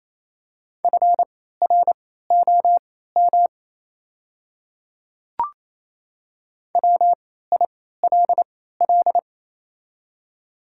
Morse code, heard as a keyed tone, about 28 wpm, spaced as for 16 wpm, 690 Hz. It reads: FROM WILL
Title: 3x Character Spacing - Top 100 Words - 28wpm